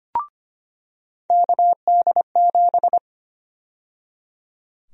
Morse code, heard as a keyed tone, about 25 wpm, 700 Hz. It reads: KD7
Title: USA Call Sign Prefixes - 25wpm